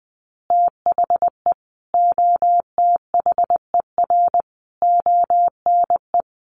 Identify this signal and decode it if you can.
Morse code, heard as a keyed tone, about 20 wpm, 700 Hz. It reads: THE OTHER ONE
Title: Additional Easy Phrases 20wpm